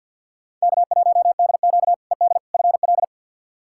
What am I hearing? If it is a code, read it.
Morse code, heard as a keyed tone, about 46 wpm, 700 Hz. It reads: K1DQ ED FL
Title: K1USN SST QSO Practice - 50wpm